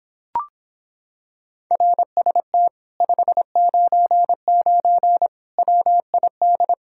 Morse code, heard as a keyed tone, about 26 wpm, 700 Hz. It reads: RST 599 WID